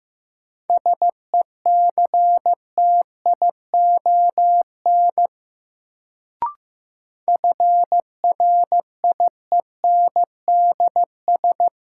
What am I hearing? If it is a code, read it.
Morse code, heard as a keyed tone, about 15 wpm, 700 Hz. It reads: SECTION FRIENDS
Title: Sentences from Top 1000 Words - Review Part 5 - New Word Drill - 15wpm